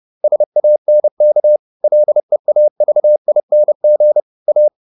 Morse code, heard as a keyed tone, about 30 wpm, 600 Hz. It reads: SANK LEAVING A